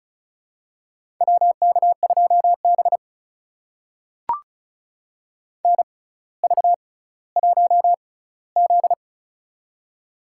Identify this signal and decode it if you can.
Morse code, heard as a keyed tone, about 35 wpm, spaced as for 18 wpm, 700 Hz. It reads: WK2B NV1Z
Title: USA Call Signs - 4 Characters - 6x Character Spacing - 35wpm